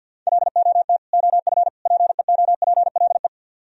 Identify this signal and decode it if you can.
Morse code, heard as a keyed tone, about 64 wpm, 700 Hz. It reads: LOT OF PEOPLE